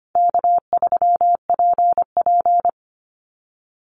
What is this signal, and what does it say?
Morse code, heard as a keyed tone, about 25 wpm, 700 Hz. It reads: K3PP
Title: Easy Call Signs Repeated 3x - 25wpm